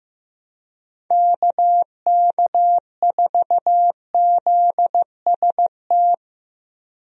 Morse code, heard as a keyed tone, about 15 wpm, 700 Hz. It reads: KK4ZST
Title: USA Call Signs - 6 Characters - Repeated 3x - 15wpm